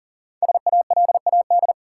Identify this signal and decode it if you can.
Morse code, heard as a keyed tone, about 40 wpm, 700 Hz. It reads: SALAD
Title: Five Letter Words - 40wpm